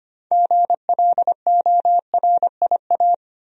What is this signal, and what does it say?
Morse code, heard as a keyed tone, about 25 wpm, 700 Hz. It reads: GLORIA
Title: USA Common Names - 25wpm